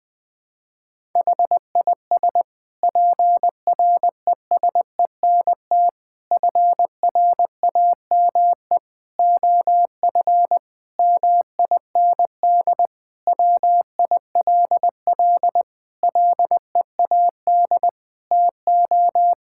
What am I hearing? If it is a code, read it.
Morse code, heard as a keyed tone, about 21 wpm, 700 Hz. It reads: HIS PRESENT FRAME OF MIND WILL LEAD TO
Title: Sentences from Top 1000 Words - Review Part 6 - 20wpm